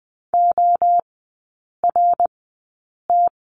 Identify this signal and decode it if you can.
Morse code, heard as a keyed tone, about 19 wpm, spaced as for 8 wpm, 690 Hz. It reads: ORT
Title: 060 Lesson 11e - Letter h - Three Characters 20wpm